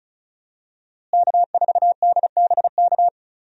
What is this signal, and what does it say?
Morse code, heard as a keyed tone, about 36 wpm, 700 Hz. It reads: K4DBK
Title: USA Call Signs - 5 Characters - Repeated 2x - 35wpm